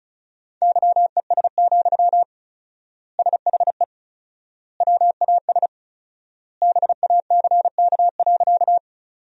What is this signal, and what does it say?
Morse code, heard as a keyed tone, about 34 wpm, 700 Hz. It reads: YES, SHE WAS BACK.
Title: 4x Word-Spacing - Sentences from Top 500 Words - 35wpm